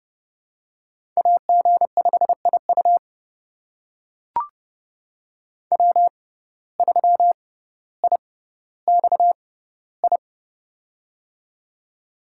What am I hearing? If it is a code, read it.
Morse code, heard as a keyed tone, about 30 wpm, spaced as for 15 wpm, 700 Hz. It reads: AG5IU W3IXI
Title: USA Call Signs - 5 Characters - 6x Character Spacing - 30wpm